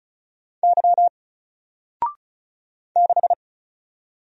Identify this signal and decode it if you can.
Morse code, heard as a keyed tone, about 34 wpm, 700 Hz. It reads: Y 6
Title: Single Character (Letter:Number:Punctuation:Prosign) 35wpm